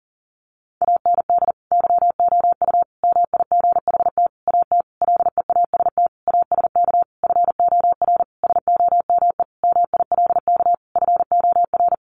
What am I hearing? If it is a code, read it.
Morse code, heard as a keyed tone, about 40 wpm, 700 Hz. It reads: AND YOU MIGHT AT LEAST ASK FOR SOME MILK FOR
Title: Sentences from Top 1500 Words 40wpm